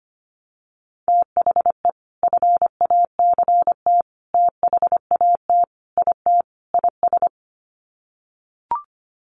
Morse code, heard as a keyed tone, about 25 wpm, 700 Hz. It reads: THE FACT THAT IT IS
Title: Sets of 5 Words Encore2 25wpm